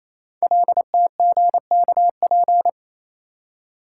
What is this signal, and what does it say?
Morse code, heard as a keyed tone, about 27 wpm, 690 Hz. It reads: LTGKP
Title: Code Groups - 5 Letter:Numbers - 28wpm